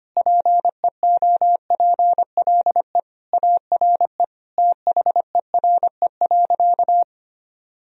Morse code, heard as a keyed tone, about 25 wpm, 700 Hz. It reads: PEOPLE ARE THERE.
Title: Sentences from Top 1000 Words - Review Part 1 - 25wpm